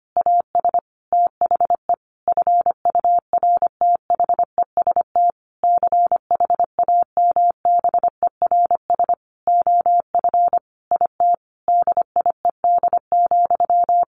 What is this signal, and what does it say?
Morse code, heard as a keyed tone, about 25 wpm, 700 Hz. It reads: AS THE FURTHEST CHAMBERS OF IT DIED,
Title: A Sound Of Thunder 25wpm